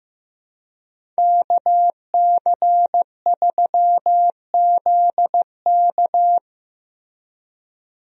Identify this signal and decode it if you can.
Morse code, heard as a keyed tone, about 15 wpm, 700 Hz. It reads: KC3ZK
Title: USA Call Signs - 5 Characters - Repeated 2x - 15wpm